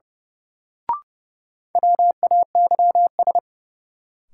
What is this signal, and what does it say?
Morse code, heard as a keyed tone, about 30 wpm, 700 Hz. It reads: WAYS
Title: Sets of 5 Words Encore - New Word Drill - 30wpm